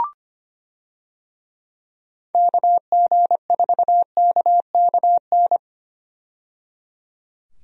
Morse code, heard as a keyed tone, about 25 wpm, 700 Hz. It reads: KG4KKN
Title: USA Call Signs - 6 Characters - 25wpm